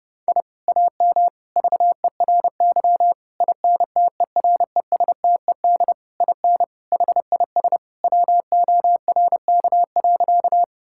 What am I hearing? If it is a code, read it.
Morse code, heard as a keyed tone, about 30 wpm, 700 Hz. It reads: I AM VERY INTERESTED IN HIS WORK.
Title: Sentences from Top 1000 Words - Review Part 6 - 30wpm